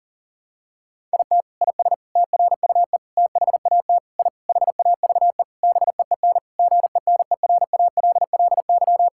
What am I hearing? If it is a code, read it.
Morse code, heard as a keyed tone, about 40 wpm, 700 Hz. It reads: IT IS TRUE THAT I HAVE BEEN GENERALLY
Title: Sentences from Top 1000 Words - Review Part 6 - 40wpm